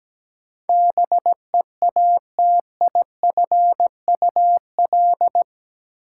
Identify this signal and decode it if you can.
Morse code, heard as a keyed tone, about 17 wpm, 700 Hz. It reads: BEATIFUL
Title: Sentences from Top 1300 Words - New Word Drill Alt - 17wpm